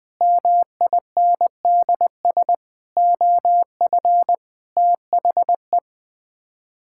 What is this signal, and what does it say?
Morse code, heard as a keyed tone, about 21 wpm, 700 Hz. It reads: MINDS OF THE
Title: Sets of 8 Words - 20wpm